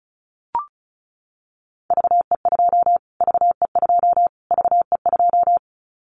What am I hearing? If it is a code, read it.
Morse code, heard as a keyed tone, about 34 wpm, 700 Hz. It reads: VE2 VE2 VE2
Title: Callsign Prefixes for Top 25 DXCC - Repeated 3x - 35wpm